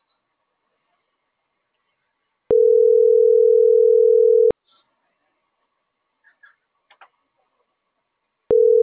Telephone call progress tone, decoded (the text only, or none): ringback tone